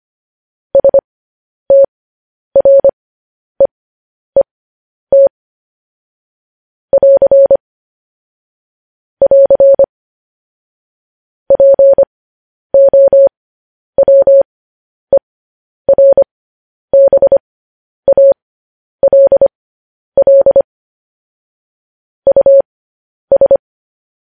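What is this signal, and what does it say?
Morse code, heard as a keyed tone, about 25 wpm, spaced as for 10 wpm, 550 Hz. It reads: STREET <AR> <AR> POWERBALL US
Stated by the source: Morse code news headlines sent at 10.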